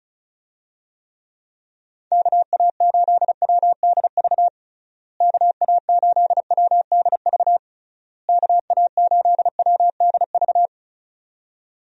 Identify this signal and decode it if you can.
Morse code, heard as a keyed tone, about 34 wpm, 700 Hz. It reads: KA8WDV KA8WDV KA8WDV
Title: USA Call Signs - 6 Characters - Repeated 3x - 35wpm